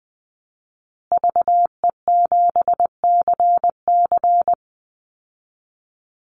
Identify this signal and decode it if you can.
Morse code, heard as a keyed tone, about 20 wpm, 700 Hz. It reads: VE7CC
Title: Call Signs Repeated 2x - 20wpm